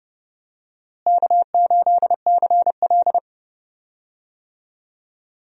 Morse code, heard as a keyed tone, about 32 wpm, 700 Hz. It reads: K8CL
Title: USA Call Signs - 4 Characters - Repeated 2x - 30wpm